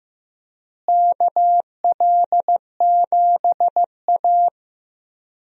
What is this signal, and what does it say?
Morse code, heard as a keyed tone, about 15 wpm, 700 Hz. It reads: KL7A
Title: DXCC-100 Call Signs - 4 Characters - Repeated 3x - 15wpm